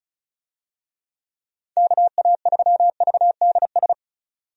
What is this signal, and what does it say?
Morse code, heard as a keyed tone, about 36 wpm, 700 Hz. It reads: KA3VDS
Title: USA Call Signs - 6 Characters - 35wpm